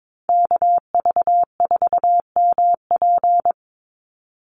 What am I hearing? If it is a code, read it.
Morse code, heard as a keyed tone, about 22 wpm, 700 Hz. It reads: KV4MP
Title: USA Call Signs 22wpm